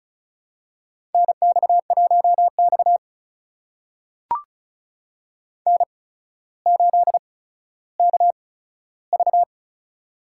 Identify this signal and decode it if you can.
Morse code, heard as a keyed tone, about 36 wpm, spaced as for 15 wpm, 700 Hz. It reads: NX1X N8KV
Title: USA Call Signs - 4 Characters - 8x Character Spacing - 35wpm